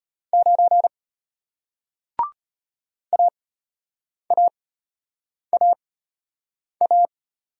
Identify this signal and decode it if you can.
Morse code, heard as a keyed tone, about 33 wpm, spaced as for 8 wpm, 700 Hz. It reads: 9 AAAA